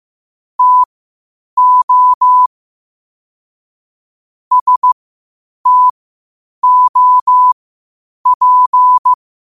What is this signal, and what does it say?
Morse code, heard as a keyed tone, about 15 wpm, spaced as for 8 wpm, 1000 Hz. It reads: TO STOP
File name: qotdfiles / 8wpm